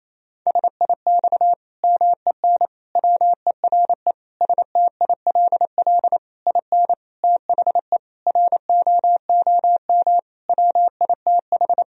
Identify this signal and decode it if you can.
Morse code, heard as a keyed tone, about 28 wpm, 700 Hz. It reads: SIX MEN WERE STILL IN THE ROOM WITH
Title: Sentences from Top 1000 Words - Review Part 2 - 28wpm